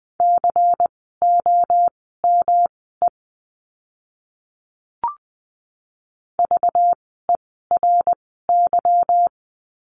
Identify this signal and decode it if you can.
Morse code, heard as a keyed tone, about 20 wpm, spaced as for 15 wpm, 700 Hz. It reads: COME VERY
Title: 244 Lesson 39f - Letter z - Words 20wpm